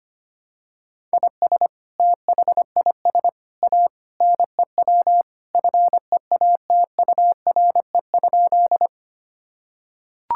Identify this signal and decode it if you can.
Morse code, heard as a keyed tone, about 25 wpm, 700 Hz. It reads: IS THIS A NEW FEATURE?
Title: Sentences from Top 1000 Words - Review Part 5 - 25wpm